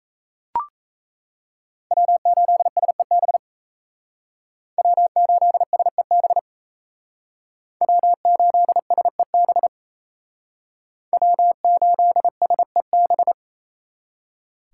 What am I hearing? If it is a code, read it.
Morse code, heard as a keyed tone, about 36 wpm, 700 Hz. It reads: W8SEB W8SEB W8SEB W8SEB